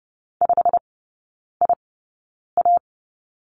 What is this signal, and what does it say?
Morse code, heard as a keyed tone, about 30 wpm, spaced as for 9 wpm, 700 Hz. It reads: 5IA
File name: Lesson 17e - Letter u - Three Characters 30wpm